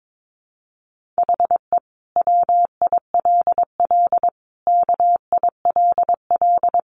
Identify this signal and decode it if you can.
Morse code, heard as a keyed tone, about 22 wpm, 700 Hz. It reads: HE WILL KILL
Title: Sentences from Top 1400 Words 22wpm